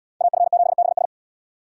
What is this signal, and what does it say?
Morse code, heard as a keyed tone, about 71 wpm, 700 Hz. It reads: AF7BS